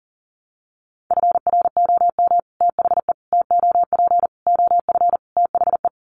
Morse code, heard as a keyed tone, about 40 wpm, 700 Hz. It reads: FROM THE TOP OF THE
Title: Sets of 5 Words Encore 40wpm